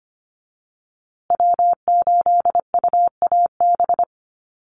Morse code, heard as a keyed tone, about 25 wpm, 700 Hz. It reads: W8UAB
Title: USA Call Signs Repeated 2x 25wpm